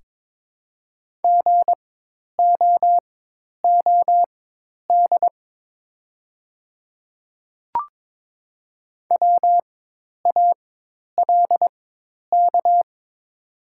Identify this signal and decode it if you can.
Morse code, heard as a keyed tone, about 22 wpm, spaced as for 10 wpm, 700 Hz. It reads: GOOD WALK
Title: 4x Character Spacing - Top 300 Words - 22wpm